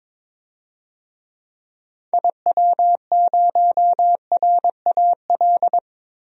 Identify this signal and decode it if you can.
Morse code, heard as a keyed tone, about 22 wpm, 700 Hz. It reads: IW0RAL
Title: DXCC-100 Call Signs - 6 Characters - 22wpm